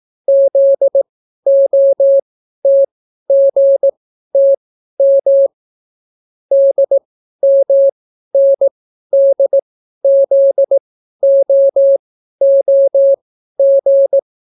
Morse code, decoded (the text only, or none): ZOTGTM DMNDZOOG